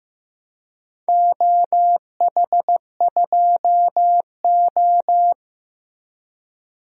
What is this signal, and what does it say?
Morse code, heard as a keyed tone, about 15 wpm, 700 Hz. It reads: OH2O
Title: DXCC-100 Call Signs - 4 Characters - 15wpm